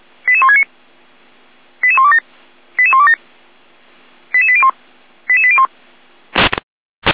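Five-tone selective call (ZVEI), heard as quick runs of five tones, tones of about 70 ms. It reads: 88169 88127 88127 88801 88801
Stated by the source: ZVEI five-tone call received on SDR.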